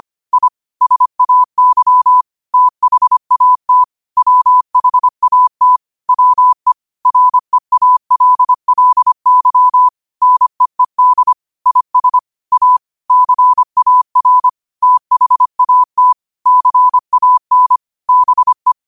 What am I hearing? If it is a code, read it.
Morse code, heard as a keyed tone, about 25 wpm, 1000 Hz. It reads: I SAY THAT WHAT WE REALLY NEED IS A CAR THAT CAN BE